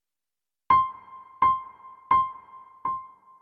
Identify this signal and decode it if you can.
There is no code.